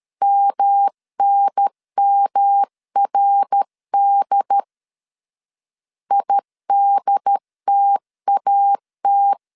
Morse code, heard as a keyed tone, about 13 wpm, 800 Hz. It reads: MNMRD IDTAT
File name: CW / DET / 1 3 1500C S A 12wpm 2905